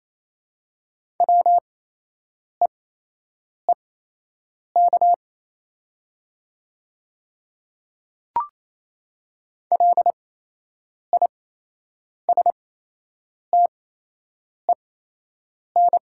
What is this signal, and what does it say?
Morse code, heard as a keyed tone, about 28 wpm, spaced as for 8 wpm, 700 Hz. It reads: WEEK LISTEN
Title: 8x Character Spacing - Top 500 Words - 28wpm